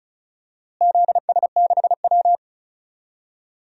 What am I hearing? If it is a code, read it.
Morse code, heard as a keyed tone, about 36 wpm, 700 Hz. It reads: ZS6W
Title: DXCC-100 Call Signs - 4 Characters - Repeated 3x - 35wpm